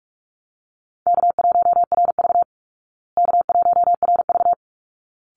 Morse code, heard as a keyed tone, about 44 wpm, 700 Hz. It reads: K1RV K1RV
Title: Easy Call Signs Repeated 2x - 45wpm